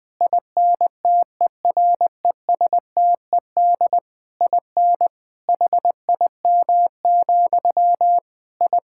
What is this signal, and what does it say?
Morse code, heard as a keyed tone, about 20 wpm, 700 Hz. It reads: INTERESTED IN HIM, I
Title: Sentences from Top 1000 Words - Review Part 6 - 20wpm